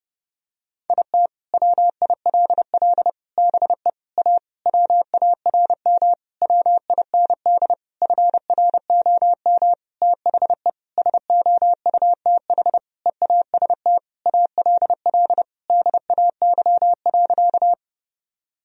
Morse code, heard as a keyed tone, about 31 wpm, 700 Hz. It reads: IT WILL BE A WARM WIND FROM THE SOUTH EAST ALL DAY.
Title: Sentences from Top 1000 Words - Review Part 3 - 30wpm